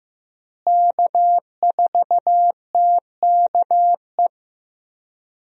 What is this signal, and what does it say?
Morse code, heard as a keyed tone, about 15 wpm, 700 Hz. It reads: K4TKE
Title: USA Call Signs - 5 Characters - Repeated 3x - 15wpm